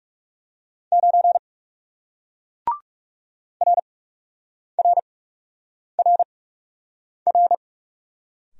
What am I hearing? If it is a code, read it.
Morse code, heard as a keyed tone, about 38 wpm, spaced as for 8 wpm, 700 Hz. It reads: 9 RRRR